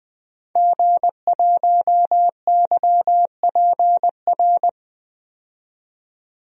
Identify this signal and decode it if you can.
Morse code, heard as a keyed tone, about 20 wpm, 700 Hz. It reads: G1YPR
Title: DXCC-100 Call Signs - 5 Characters - 20wpm